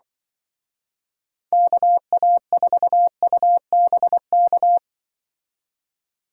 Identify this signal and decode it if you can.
Morse code, heard as a keyed tone, about 25 wpm, 700 Hz. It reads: KA4UBK